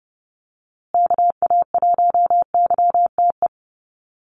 Morse code, heard as a keyed tone, about 30 wpm, 700 Hz. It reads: KA1YTE
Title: USA Call Signs Repeated 3x 30wpm